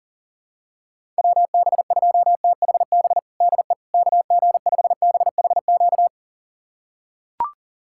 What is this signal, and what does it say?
Morse code, heard as a keyed tone, about 40 wpm, 700 Hz. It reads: WB2THB DE KG5BHQ
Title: USA Callsign DE Callsign 40wpm